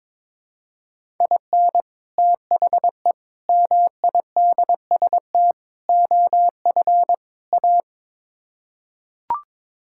Morse code, heard as a keyed tone, about 22 wpm, 700 Hz. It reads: IN THE MIDST OF A
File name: practice / 22wpm / Build A Phrase - 5-Word Phrases - 22wpm